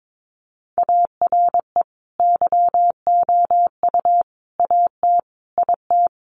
Morse code, heard as a keyed tone, about 22 wpm, 700 Hz. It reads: ARE YOU AT IT
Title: Sentences from Top 200 Words 22wpm